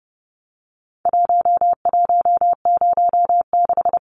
Morse code, heard as a keyed tone, about 30 wpm, 700 Hz. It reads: 1106
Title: Numbers - Contest Style - 4-Digit 30wpm